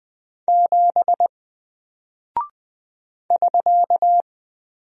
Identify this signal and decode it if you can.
Morse code, heard as a keyed tone, about 20 wpm, 700 Hz. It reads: 7 <SK>
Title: Single Character (Letter:Number:Punctuation:Prosign) 20wpm